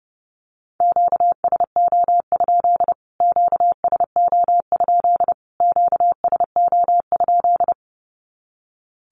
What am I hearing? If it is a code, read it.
Morse code, heard as a keyed tone, about 30 wpm, 700 Hz. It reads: QSO? QSO? QSO?
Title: Q-Codes Repeated 3x 30wpm